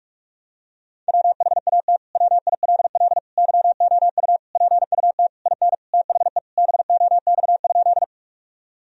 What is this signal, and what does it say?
Morse code, heard as a keyed tone, about 41 wpm, 700 Hz. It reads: WHAT WILL YOU PUT IN THE BOX?
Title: Sentences from Top 1000 Words - Review Part 2 - 45wpm